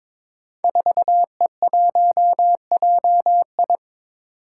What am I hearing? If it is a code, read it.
Morse code, heard as a keyed tone, about 22 wpm, 700 Hz. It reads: 4E1JI
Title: Five Challenging Letter-Number Combinations 22wpm